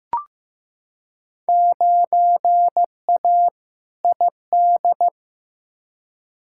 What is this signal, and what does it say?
Morse code, heard as a keyed tone, about 15 wpm, 700 Hz. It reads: 9A ID
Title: Field Day Exchanges Encore - 15wpm